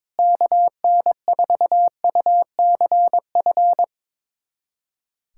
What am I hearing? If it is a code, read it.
Morse code, heard as a keyed tone, about 22 wpm, 700 Hz. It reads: KN4UCF
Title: USA Callsign DE Callsign 22wpm